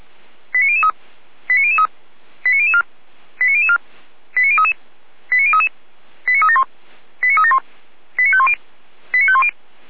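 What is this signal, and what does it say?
Five-tone selective call (ZVEI), heard as quick runs of five tones, tones of about 70 ms. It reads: 89003 89003 89004 89004 89033 89033 89371 89371 89510 89510
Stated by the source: ZVEI five-tone call received by SDR, from the Feuerwehr Orlamünde audio server.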